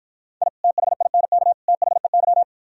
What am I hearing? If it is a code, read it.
Morse code, heard as a keyed tone, about 53 wpm, 690 Hz. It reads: I THINK THEY